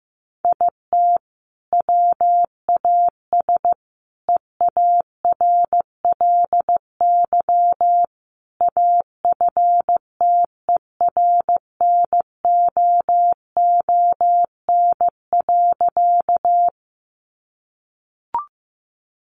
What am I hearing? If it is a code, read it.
Morse code, heard as a keyed tone, about 15 wpm, 700 Hz. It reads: IT WAS EARLY AFTERNOON.
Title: Sentences from Top 1500 Words 15wpm